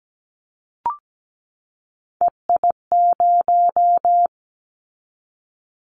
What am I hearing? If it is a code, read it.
Morse code, heard as a keyed tone, about 17 wpm, 700 Hz. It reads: EI0
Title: Callsign Prefixes for Top 50 DXCC - 17wpm